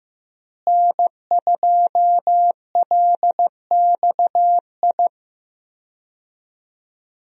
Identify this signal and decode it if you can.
Morse code, heard as a keyed tone, about 15 wpm, 700 Hz. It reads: N2LXI
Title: USA Call Signs - 5 Characters - Repeated 3x - 15wpm